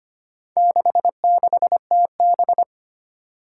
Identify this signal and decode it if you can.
Morse code, heard as a keyed tone, about 25 wpm, 700 Hz. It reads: 66TB
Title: Four Challenging Letter-Number Combinations 25wpm